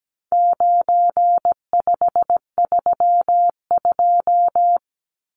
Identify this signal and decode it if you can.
Morse code, heard as a keyed tone, about 17 wpm, 700 Hz. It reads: 9532
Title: Numbers - 4-Digit 17wpm